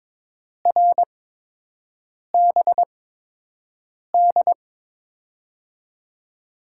Morse code, heard as a keyed tone, about 23 wpm, spaced as for 6 wpm, 700 Hz. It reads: RBD
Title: USA Call Signs - 8x Character Spacing - 22wpm